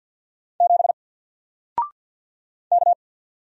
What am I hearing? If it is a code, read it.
Morse code, heard as a keyed tone, about 48 wpm, 700 Hz. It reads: 7 K
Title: Single Letter-Number Alt 50wpm